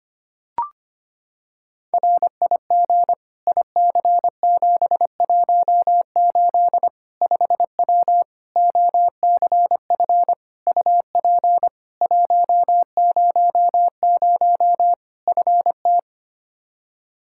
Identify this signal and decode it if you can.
Morse code, heard as a keyed tone, about 25 wpm, 700 Hz. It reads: RIG IC718 5W OCF UP 100 FT
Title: K1USN SST QSO Practice 25wpm